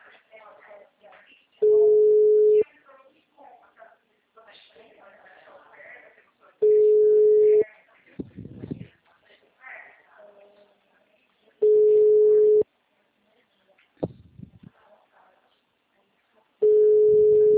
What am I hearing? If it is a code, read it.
Ringback tone (Europe, 425 Hz), with speech.